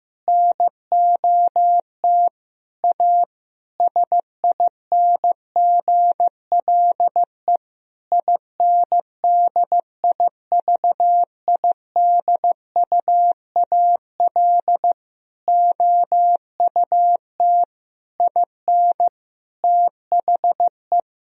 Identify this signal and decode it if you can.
Morse code, heard as a keyed tone, about 15 wpm, 700 Hz. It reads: NOT A SINGLE INDIVIDUAL OUT IN THE
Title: Sentences from Top 1000 Words - Review Part 4 - 15wpm